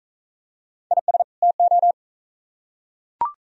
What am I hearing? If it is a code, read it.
Morse code, heard as a keyed tone, about 40 wpm, 700 Hz. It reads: IS TO